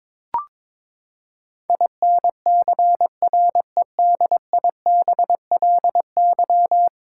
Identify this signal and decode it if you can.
Morse code, heard as a keyed tone, about 22 wpm, 700 Hz. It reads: INCREDIBLY
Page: morse-code-ninja-practice-22wpm